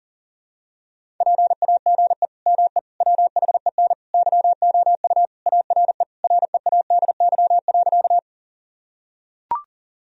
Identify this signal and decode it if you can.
Morse code, heard as a keyed tone, about 37 wpm, 700 Hz. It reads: PAGE ME WHEN YOU ARE READY.
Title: Sentences from Top 1000 Words - Review Part 1 - 40wpm